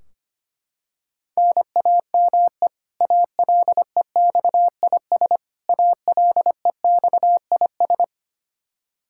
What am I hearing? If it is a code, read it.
Morse code, heard as a keyed tone, about 25 wpm, 700 Hz. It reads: NAME ALEXIS ALEXIS
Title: K1USN SST QSO Practice 25wpm